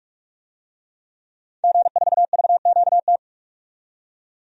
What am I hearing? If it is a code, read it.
Morse code, heard as a keyed tone, about 46 wpm, 700 Hz. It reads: G4VQT
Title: DXCC-100 Call Signs - 5 Characters - 45wpm